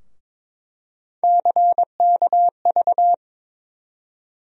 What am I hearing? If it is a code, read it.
Morse code, heard as a keyed tone, about 22 wpm, 700 Hz. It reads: CKV
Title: USA Call Sign Postfixes - 3 Letters 22wpm